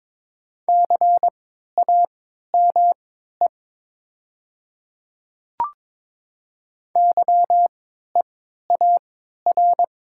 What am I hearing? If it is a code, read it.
Morse code, heard as a keyed tone, about 22 wpm, spaced as for 12 wpm, 700 Hz. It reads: CAME YEAR
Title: 3x Character Spacing - Top 500 Words - 22wpm